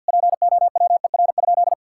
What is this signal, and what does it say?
Morse code, heard as a keyed tone, about 48 wpm, 700 Hz. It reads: POWER?
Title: Sentences from Top 1000 Words - Review Part 5 - 50wpm